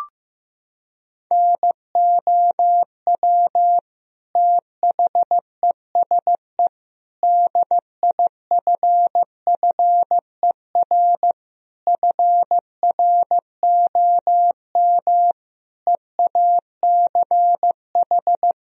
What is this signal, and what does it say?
Morse code, heard as a keyed tone, about 15 wpm, 700 Hz. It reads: NOW THESE DIFFER FROM EACH
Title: Sentences from Top 1000 Words - Review Part 1 - 15wpm